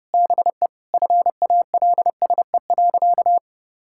Morse code, heard as a keyed tone, about 30 wpm, 700 Hz. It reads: BE FALSE.
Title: Sentences from Top 1600 Words 30wpm